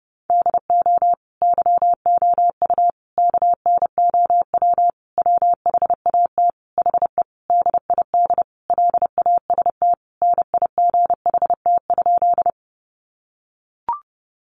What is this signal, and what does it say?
Morse code, heard as a keyed tone, about 30 wpm, 700 Hz. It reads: DO YOU KNOW WHAT HE DID LAST NIGHT?
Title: Sentences from Top 300 Words 30wpm